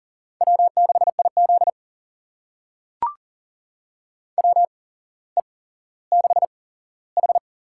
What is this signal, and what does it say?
Morse code, heard as a keyed tone, about 40 wpm, spaced as for 17 wpm, 700 Hz. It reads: W6IZ WE6H
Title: USA Call Signs - 4 Characters - 8x Character Spacing - 40wpm